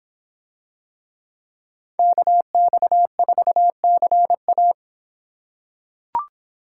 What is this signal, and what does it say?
Morse code, heard as a keyed tone, about 26 wpm, 700 Hz. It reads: KX4CA